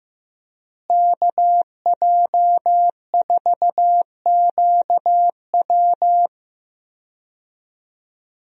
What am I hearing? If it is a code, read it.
Morse code, heard as a keyed tone, about 15 wpm, 700 Hz. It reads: KJ4QW
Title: USA Call Signs - 5 Characters - Repeated 3x - 15wpm